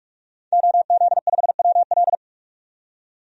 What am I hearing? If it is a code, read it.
Morse code, heard as a keyed tone, about 46 wpm, 700 Hz. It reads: OZ5WL
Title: DXCC-100 Call Signs - 5 Characters - 45wpm